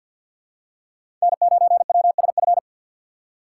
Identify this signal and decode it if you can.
Morse code, heard as a keyed tone, about 53 wpm, 700 Hz. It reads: N9WSF
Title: USA Call Signs - 5 Characters - 2x Character Spacing - 50wpm